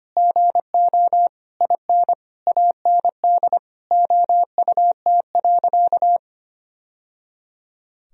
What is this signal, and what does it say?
Morse code, heard as a keyed tone, about 25 wpm, 700 Hz. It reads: GO IN AND OUT.
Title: Sentences from Top 1000 Words - Review Part 2 - 25wpm